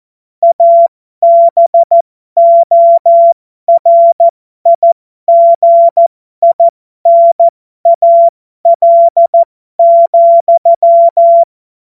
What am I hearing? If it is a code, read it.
Morse code, heard as a keyed tone, about 14 wpm, spaced as for 12 wpm, 680 Hz. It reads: ABORIGINAL,
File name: Morsekurs / Witze